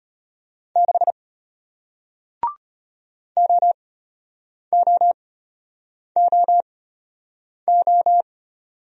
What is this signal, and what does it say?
Morse code, heard as a keyed tone, about 33 wpm, spaced as for 8 wpm, 700 Hz. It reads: 6 OOOO